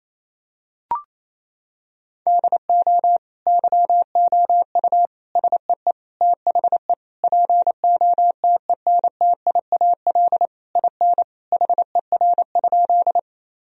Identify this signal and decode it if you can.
Morse code, heard as a keyed tone, about 28 wpm, 700 Hz. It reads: DO YOU SEE THE POTENTIAL IN HER?
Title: Sentences from Top 1000 Words - Review Part 4 - 28wpm